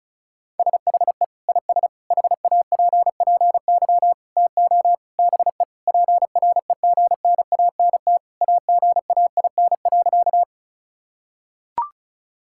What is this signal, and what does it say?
Morse code, heard as a keyed tone, about 34 wpm, 700 Hz. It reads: SHE IS HAPPY TO BE PREGNANT AGAIN.
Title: Sentences from Top 1000 Words - Review Part 6 - 35wpm